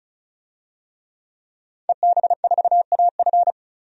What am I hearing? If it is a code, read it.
Morse code, heard as a keyed tone, about 36 wpm, 700 Hz. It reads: EB4AF
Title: DXCC-100 Call Signs - 5 Characters - 35wpm